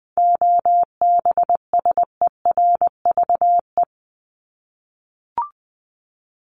Morse code, heard as a keyed tone, about 20 wpm, 700 Hz. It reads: OBSERVE
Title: ED Suffix - Top 100 Words with Base Words 20wpm